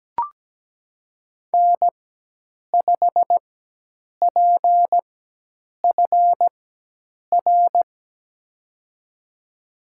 Morse code, heard as a keyed tone, about 17 wpm, spaced as for 8 wpm, 700 Hz. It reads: N5PFR
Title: USA Call Signs - 5 Characters - 4x Character Spacing - 17wpm